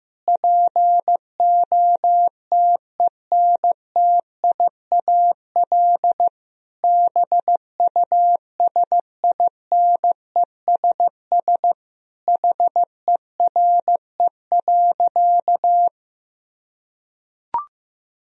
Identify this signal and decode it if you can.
Morse code, heard as a keyed tone, about 15 wpm, 700 Hz. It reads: POTENTIAL BUSINESS HERE.
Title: Sentences from Top 1000 Words - Review Part 4 - 15wpm